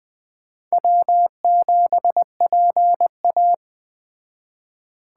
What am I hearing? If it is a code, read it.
Morse code, heard as a keyed tone, about 21 wpm, 700 Hz. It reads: W7PA
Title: Speed Racing - USA Call Signs 17wpm